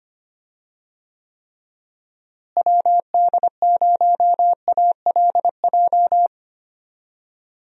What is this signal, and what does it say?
Morse code, heard as a keyed tone, about 25 wpm, 700 Hz. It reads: WD0ALJ
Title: USA Call Signs - 6 Characters - Repeated 3x - 25wpm